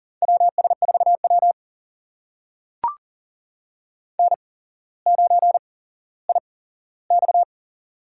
Morse code, heard as a keyed tone, about 38 wpm, spaced as for 17 wpm, 700 Hz. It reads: WS4W N9IX
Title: USA Call Signs - 4 Characters - 8x Character Spacing - 40wpm